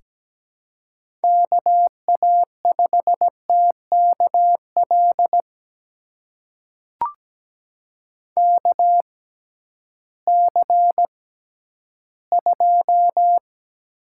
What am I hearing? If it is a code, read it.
Morse code, heard as a keyed tone, about 17 wpm, spaced as for 11 wpm, 700 Hz. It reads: KA5TKL KC2
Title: USA Call Signs - 6 Characters - 6x Character Spacing - 17wpm